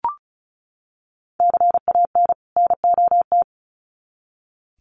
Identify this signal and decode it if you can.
Morse code, heard as a keyed tone, about 34 wpm, 700 Hz. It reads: CAN NOT
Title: Sets of 2 Words 35wpm